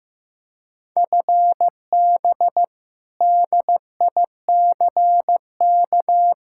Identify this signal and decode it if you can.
Morse code, heard as a keyed tone, about 15 wpm, 700 Hz. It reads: FB DICK
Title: Practice QSO - Exchange 3 - More Info - 15wpm